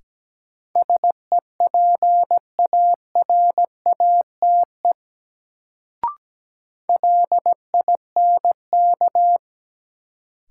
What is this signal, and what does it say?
Morse code, heard as a keyed tone, about 17 wpm, 700 Hz. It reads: SEPARATE LINK
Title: Sentences from Top 1000 Words - Review Part 7 - New Word Drill - 17wpm